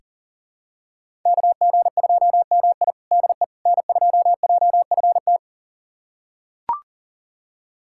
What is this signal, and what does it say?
Morse code, heard as a keyed tone, about 40 wpm, 700 Hz. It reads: KG2MI DE N2JFT
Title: USA Callsign DE Callsign 40wpm